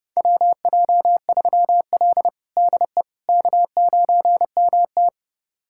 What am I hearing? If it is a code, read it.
Morse code, heard as a keyed tone, about 32 wpm, 700 Hz. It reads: WJ3L DE K9MT
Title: USA Callsign DE Callsign - 4 Characters - 30wpm